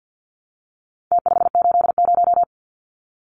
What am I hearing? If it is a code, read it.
Morse code, heard as a keyed tone, about 49 wpm, 700 Hz. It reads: T580
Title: Numbers - Contest Style - 4-Digit 50wpm